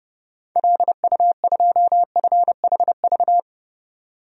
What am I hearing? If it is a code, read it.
Morse code, heard as a keyed tone, about 32 wpm, 700 Hz. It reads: LU2FHV
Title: DXCC-100 Call Signs - 6 Characters - 30wpm